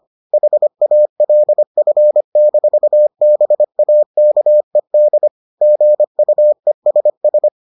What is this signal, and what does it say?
Morse code, heard as a keyed tone, about 25 wpm, 600 Hz. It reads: HALF-BAKED GUESS